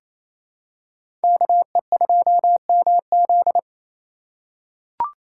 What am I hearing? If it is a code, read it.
Morse code, heard as a keyed tone, about 28 wpm, 700 Hz. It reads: KE2MZ
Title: USA Call Signs - 5 Characters - 4x Character Spacing - 28wpm